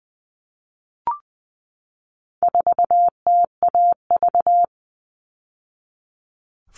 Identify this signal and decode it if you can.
Morse code, heard as a keyed tone, about 20 wpm, 700 Hz. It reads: 4TAV
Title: All Cut Numbers - 4-Digit 20wpm